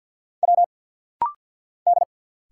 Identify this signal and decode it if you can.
Morse code, heard as a keyed tone, about 48 wpm, 700 Hz. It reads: W D
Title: Single Letters - Rapid-Fire 50wpm